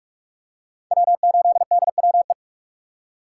Morse code, heard as a keyed tone, about 46 wpm, 700 Hz. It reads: W8DWE
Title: USA Call Signs - 2x Character Spacing - 45wpm